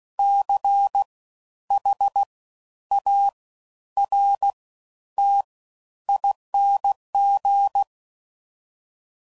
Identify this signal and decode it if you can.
Morse code, heard as a keyed tone, about 16 wpm, spaced as for 10 wpm, 800 Hz. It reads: CHARTING
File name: Suffixes / Special INGs / Fast Ings